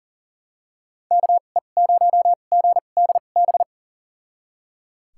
Morse code, heard as a keyed tone, about 39 wpm, spaced as for 28 wpm, 700 Hz. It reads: KE0GDB
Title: USA Call Signs - 6 Characters - 2x Character Spacing - 40wpm